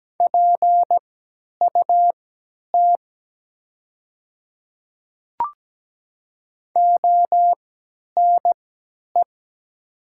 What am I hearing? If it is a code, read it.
Morse code, heard as a keyed tone, about 17 wpm, spaced as for 10 wpm, 700 Hz. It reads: PUT ONE
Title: 3x Character Spacing - Top 200 Words - 17wpm